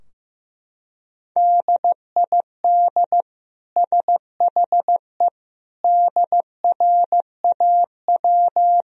Morse code, heard as a keyed tone, about 15 wpm, 700 Hz. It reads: DID SHE DRAW
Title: Sentences from Top 1000 Words - Review Part 2 - 15wpm